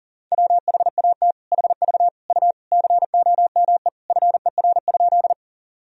Morse code, heard as a keyed tone, about 40 wpm, 700 Hz. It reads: WHAT HV U COME FER?
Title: Sentences from Top 100 Words and Abbreviations 40wpm